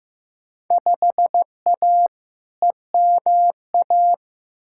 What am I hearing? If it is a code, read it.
Morse code, heard as a keyed tone, about 15 wpm, 700 Hz. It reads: 5A EMA
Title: Field Day Exchanges Encore - 15wpm